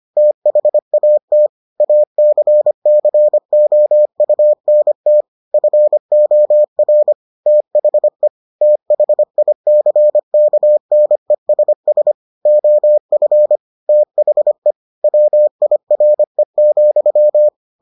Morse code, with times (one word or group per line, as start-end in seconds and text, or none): 0.2-1.5 THAT
1.8-5.2 ACCOUNT
5.5-7.1 FOR
7.5-8.3 THE
8.6-12.1 THICKNESS
12.5-13.6 OF
13.9-14.7 THE
15.0-17.5 WIRE,